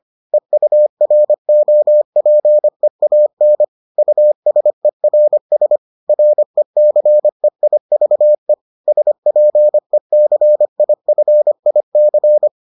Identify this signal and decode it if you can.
Morse code, heard as a keyed tone, about 25 wpm, 600 Hz. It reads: EUROPEAN USERS RECEIVE SPECIFIC